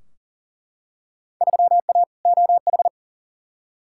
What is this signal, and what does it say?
Morse code, heard as a keyed tone, about 40 wpm, 700 Hz. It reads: 3A OH
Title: Field Day Exchanges - 40wpm